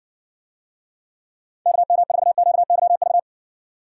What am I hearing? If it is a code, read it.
Morse code, heard as a keyed tone, about 60 wpm, 700 Hz. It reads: KM4QYV